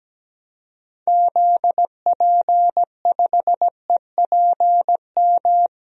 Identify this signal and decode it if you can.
Morse code, heard as a keyed tone, about 17 wpm, 700 Hz. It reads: ZP5EPM
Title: DXCC-100 Call Signs - 6 Characters - Repeated 2x - 17wpm